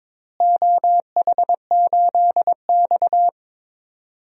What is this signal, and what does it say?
Morse code, heard as a keyed tone, about 22 wpm, 700 Hz. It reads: OH8X
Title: DXCC-100 Call Signs - 4 Characters - Repeated 2x - 22wpm